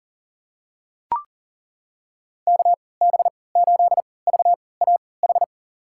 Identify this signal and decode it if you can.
Morse code, heard as a keyed tone, about 40 wpm, spaced as for 23 wpm, 700 Hz. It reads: KB8VAH
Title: USA Call Signs - 3x Character Spacing - 40wpm